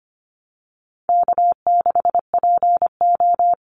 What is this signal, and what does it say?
Morse code, heard as a keyed tone, about 25 wpm, 700 Hz. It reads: K6PO
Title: Call Signs Repeated 3x - 25wpm